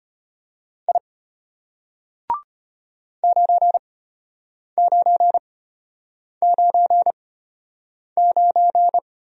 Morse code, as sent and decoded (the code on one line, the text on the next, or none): .. / ----. ----. ----. ----.
I 9999